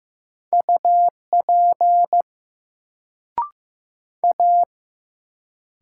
Morse code, heard as a keyed tone, about 15 wpm, 700 Hz. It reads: UP A
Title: Sentences from Top 100 Words - New Word Drill - 15wpm